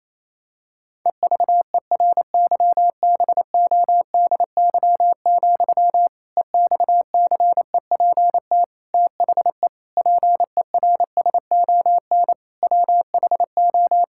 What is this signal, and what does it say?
Morse code, heard as a keyed tone, about 28 wpm, 700 Hz. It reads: EVERYBODY, EXCEPT THE PERSON WHO
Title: Sentences from Top 1600 Words 28wpm